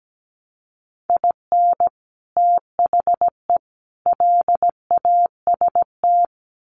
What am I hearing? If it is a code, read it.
Morse code, heard as a keyed tone, about 17 wpm, 700 Hz. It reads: IN THE LAST
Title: Sets of 3 Words 17wpm